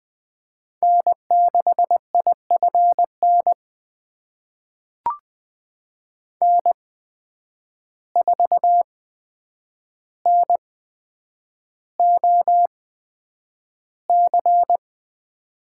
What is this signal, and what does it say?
Morse code, heard as a keyed tone, about 20 wpm, spaced as for 9 wpm, 700 Hz. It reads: N6IFN N4NOC
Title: USA Call Signs - 5 Characters - 8x Character Spacing - 20wpm